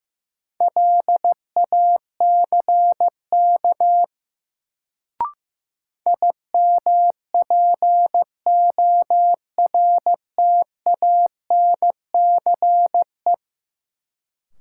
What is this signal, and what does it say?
Morse code, heard as a keyed tone, about 15 wpm, 700 Hz. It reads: LACK IMPORTANCE
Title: Sets of 3 Words - New Word Drill - 15wpm